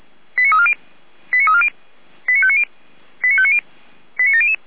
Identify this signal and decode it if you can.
Five-tone selective call (ZVEI), heard as quick runs of five tones, tones of about 70 ms. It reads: 89340 89340 89590 89590 89700
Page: /455/SDR/ZVEI/